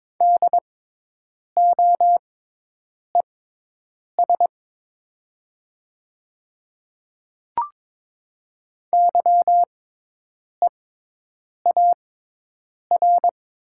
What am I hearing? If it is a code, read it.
Morse code, heard as a keyed tone, about 22 wpm, spaced as for 8 wpm, 700 Hz. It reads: DOES YEAR
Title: 6x Character Spacing - Top 300 Words - 22wpm